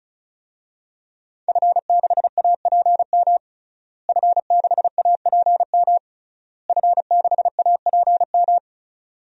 Morse code, heard as a keyed tone, about 36 wpm, 700 Hz. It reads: F6APM F6APM F6APM
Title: DXCC-100 Call Signs - 5 Characters - Repeated 3x - 35wpm